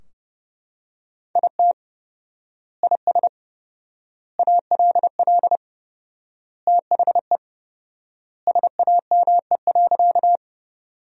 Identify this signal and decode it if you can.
Morse code, heard as a keyed tone, about 30 wpm, 700 Hz. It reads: IT IS ALL THE SAME.
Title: 4x Word-Spacing - Sentences from Top 200 Words - 30wpm